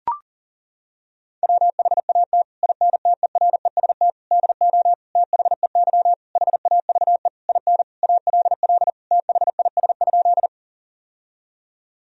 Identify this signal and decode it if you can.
Morse code, heard as a keyed tone, about 40 wpm, 700 Hz. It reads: WHAT INTEREST DO THEY HAVE IN ALL THIS?
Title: Sentences from Top 1000 Words - Review Part 2 - 40wpm